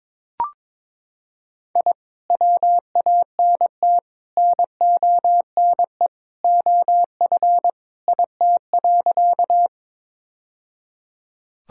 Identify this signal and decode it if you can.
Morse code, heard as a keyed tone, about 22 wpm, 700 Hz. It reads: I WANT NONE OF IT.
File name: practice / 22wpm / Sentences with 2-Word Phrases - 22wpm